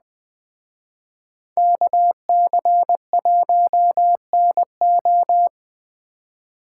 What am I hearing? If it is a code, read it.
Morse code, heard as a keyed tone, about 20 wpm, 700 Hz. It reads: KC1NO
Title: Speed Racing - USA 2x2 Call Signs 17wpm